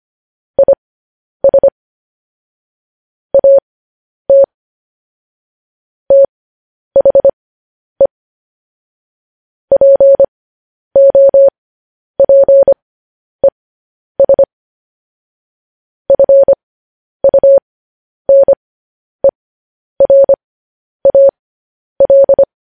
Morse code, heard as a keyed tone, about 25 wpm, spaced as for 10 wpm, 550 Hz. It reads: IS AT THE POPES FUNERAL
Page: news headlines in morse code at 10 wpm